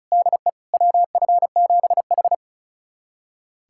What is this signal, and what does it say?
Morse code, heard as a keyed tone, about 36 wpm, 700 Hz. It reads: DE WF7H
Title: USA Callsign DE Callsign - 4 Characters - 35wpm